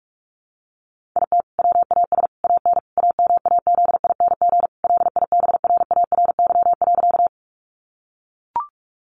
Morse code, heard as a keyed tone, about 44 wpm, 700 Hz. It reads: IT WAS AN AMAZING LIBRARY.
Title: Sentences from Top 1500 Words 45wpm